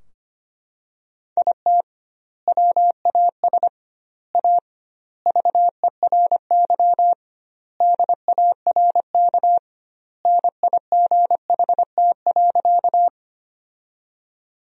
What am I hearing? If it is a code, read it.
Morse code, heard as a keyed tone, about 25 wpm, 700 Hz. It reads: IT WAS A VERY DARK NIGHT.
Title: 2x Word-Spacing - Sentences from Top 500 Words - 25wpm